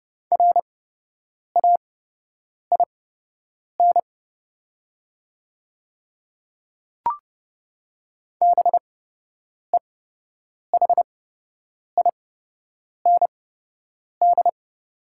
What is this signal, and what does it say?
Morse code, heard as a keyed tone, about 32 wpm, spaced as for 8 wpm, 700 Hz. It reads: RAIN BEHIND
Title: 8x Character Spacing - Top 500 Words - 30wpm